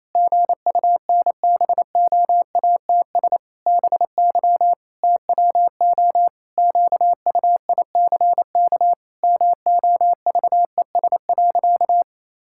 Morse code, heard as a keyed tone, about 28 wpm, 700 Hz. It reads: GUNBOATS BY TWO QUICK MOVES.
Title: Pangram Challenge v2 - 28wpm